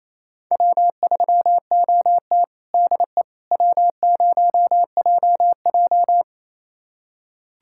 Morse code, heard as a keyed tone, about 28 wpm, 700 Hz. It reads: W3OT DE W0JJ
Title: USA Callsign DE Callsign - 4 Characters - 28wpm